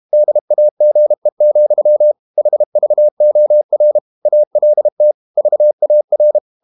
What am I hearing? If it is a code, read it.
Morse code, heard as a keyed tone, about 32 wpm, 600 Hz. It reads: DAGE, HVOR ALT VAR
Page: Ferie 32 wpm